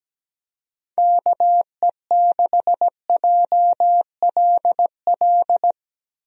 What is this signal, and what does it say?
Morse code, heard as a keyed tone, about 17 wpm, 700 Hz. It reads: KE6JLL
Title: USA Call Signs - 6 Characters - Repeated 3x - 17wpm